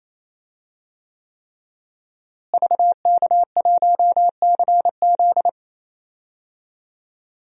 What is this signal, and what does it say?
Morse code, heard as a keyed tone, about 28 wpm, 700 Hz. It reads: VK1CZ
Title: DXCC-100 Call Signs - 5 Characters - Repeated 3x - 28wpm